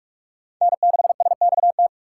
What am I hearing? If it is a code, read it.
Morse code, heard as a keyed tone, about 42 wpm, 700 Hz. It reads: N6SXT